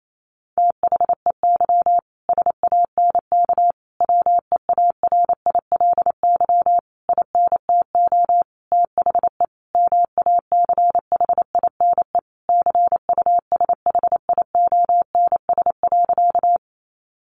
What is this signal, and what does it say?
Morse code, heard as a keyed tone, about 28 wpm, 700 Hz. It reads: THEY SANK WEARILY INTO THE MACHINE CUSHIONS.
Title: A Sound Of Thunder 28wpm